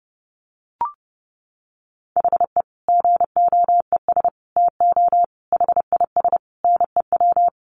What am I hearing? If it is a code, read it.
Morse code, heard as a keyed tone, about 30 wpm, 700 Hz. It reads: HE GOES TO HIS NEW
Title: Easy Phrases 30wpm